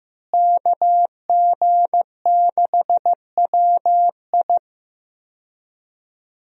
Morse code, heard as a keyed tone, about 15 wpm, 700 Hz. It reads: KG6WI
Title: USA Call Signs - 5 Characters - Repeated 3x - 15wpm